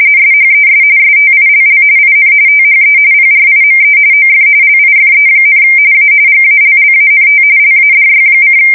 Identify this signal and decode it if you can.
RTTY, about 45.45 baud. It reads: SHES STILL IN THE PRIME OF LIFE, A GALLANT LADY:
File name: fsk170 / USS-New-Jersey